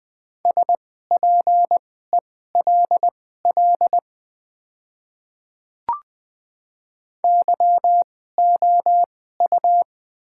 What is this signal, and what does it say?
Morse code, heard as a keyed tone, about 20 wpm, spaced as for 14 wpm, 700 Hz. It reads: SPELL YOU
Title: 2x Character Spacing - Top 200 Words - 20wpm